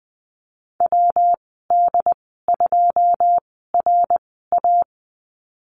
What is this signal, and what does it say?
Morse code, heard as a keyed tone, about 20 wpm, spaced as for 14 wpm, 700 Hz. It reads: WD2RA
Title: 161 Lesson 26g - Letter g - Call Signs 20wpm